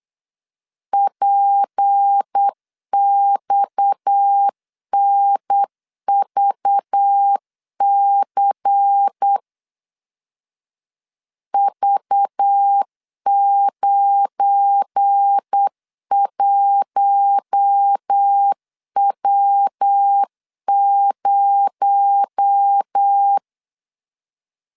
Morse code, heard as a keyed tone, about 8 wpm, 800 Hz. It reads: PXNVC V91W0
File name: LEZIONI / 8 YJ45 / 8 YJ45 8wpm 3214